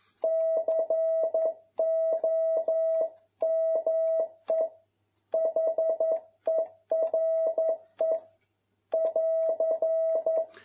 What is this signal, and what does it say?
Morse code, heard as a keyed tone, about 11 wpm, 650 Hz. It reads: COME HERE <AR>
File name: jh8cbh / 2021cw